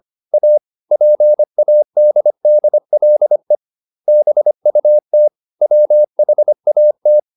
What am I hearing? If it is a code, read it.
Morse code, heard as a keyed tone, about 25 wpm, 600 Hz. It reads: A PADDLE BUT WHAT